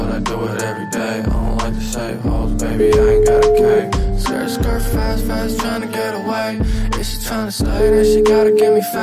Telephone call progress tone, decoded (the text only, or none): ringback tone